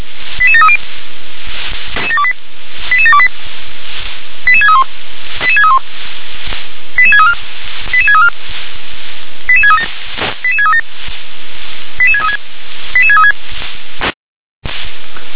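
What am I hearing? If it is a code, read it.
Five-tone selective call (ZVEI), heard as quick runs of five tones, tones of about 70 ms. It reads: 88730 88728 88631 88631 88634 88634 88637 88637 88636 88636
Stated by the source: ZVEI five-tone selective call received by SDR.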